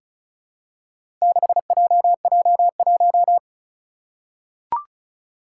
Morse code, heard as a keyed tone, about 36 wpm, 700 Hz. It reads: 6JJ1
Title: Four Challenging Letter-Number Combinations 35wpm